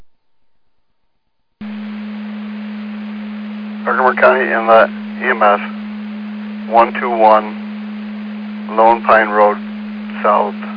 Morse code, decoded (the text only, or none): none